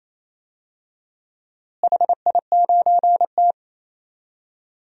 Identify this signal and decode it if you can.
Morse code, heard as a keyed tone, about 28 wpm, 700 Hz. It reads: HI9T
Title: DXCC-100 Call Signs - 4 Characters - Repeated 2x - 28wpm